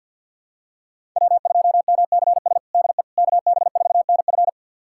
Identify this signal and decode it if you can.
Morse code, heard as a keyed tone, about 51 wpm, 700 Hz. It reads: W2MCS DE KB4NF
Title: USA Callsign DE Callsign - 5 Characters - 50wpm